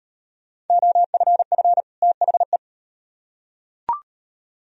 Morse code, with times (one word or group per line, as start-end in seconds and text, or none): 0.7-1.8 OFF
2.0-2.6 THE